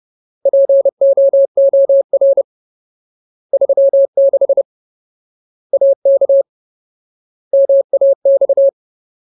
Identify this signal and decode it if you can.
Morse code, heard as a keyed tone, about 30 wpm, 550 Hz. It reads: POOR 36 AK MAX
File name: Ultimate Word List / 30WPM Ultimate